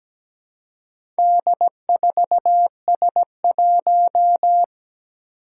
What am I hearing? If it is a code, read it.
Morse code, heard as a keyed tone, about 17 wpm, 700 Hz. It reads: D4S1
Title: Four Challenging Letter-Number Combinations 17wpm